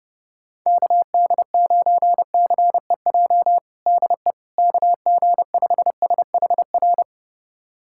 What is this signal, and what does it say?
Morse code, heard as a keyed tone, about 30 wpm, 700 Hz. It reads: KD9CEJ DE KG5SHR
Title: USA Callsign DE Callsign - 6 Characters - 30wpm